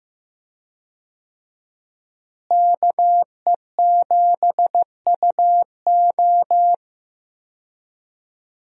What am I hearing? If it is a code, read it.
Morse code, heard as a keyed tone, about 16 wpm, 700 Hz. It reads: KE7UO
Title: USA Call Signs - 5 Characters - Repeated 3x - 15wpm